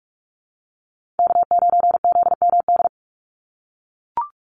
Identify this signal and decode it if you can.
Morse code, heard as a keyed tone, about 44 wpm, 690 Hz. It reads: K9ZMD
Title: NA SOTA Chaser Callsigns 45wpm